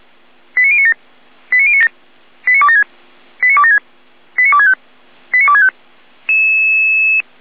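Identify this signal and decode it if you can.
Five-tone selective call (ZVEI), heard as quick runs of five tones, tones of about 70 ms. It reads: 89097 89097 89276 89276 89265 89265
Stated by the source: ZVEI five-tone call received by SDR.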